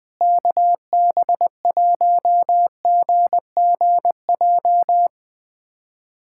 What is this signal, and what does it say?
Morse code, heard as a keyed tone, about 20 wpm, 700 Hz. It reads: KB1GGJ
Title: USA Callsign DE Callsign - 6 Characters - 20wpm